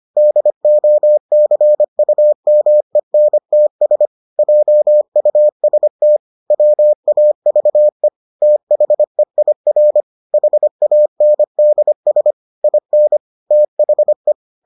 Morse code, heard as a keyed tone, about 25 wpm, 600 Hz. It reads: DOCUMENTS JUST WAVE THEIR HANDS IN THE